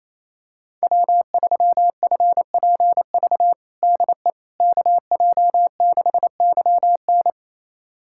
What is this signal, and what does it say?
Morse code, heard as a keyed tone, about 28 wpm, 700 Hz. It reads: W3FPV DE KJ6YN
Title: USA Callsign DE Callsign - 5 Characters - 28wpm